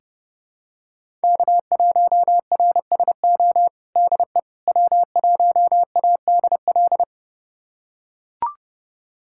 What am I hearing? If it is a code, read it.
Morse code, heard as a keyed tone, about 30 wpm, 700 Hz. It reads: K1RSO DE W1ADL
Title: USA Callsign DE Callsign - 5 Characters - 30wpm